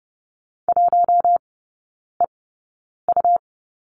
Morse code, heard as a keyed tone, about 30 wpm, spaced as for 9 wpm, 700 Hz. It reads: 1EU
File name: Lesson 17e - Letter u - Three Characters 30wpm